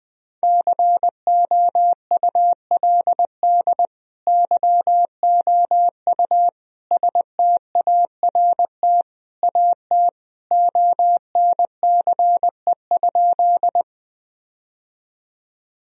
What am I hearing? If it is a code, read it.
Morse code, heard as a keyed tone, about 20 wpm, 700 Hz. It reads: COULD YOU START AT ONCE?
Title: Sentences from Top 1000 Words - Review Part 1 - 20wpm